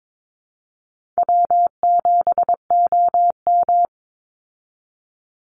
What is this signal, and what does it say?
Morse code, heard as a keyed tone, about 22 wpm, 700 Hz. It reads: W7OM
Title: Easy Call Signs Repeated 2x - 22wpm